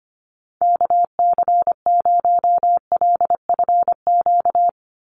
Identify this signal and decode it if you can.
Morse code, heard as a keyed tone, about 25 wpm, 690 Hz. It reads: KC0LFQ
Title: USA Call Signs 25wpm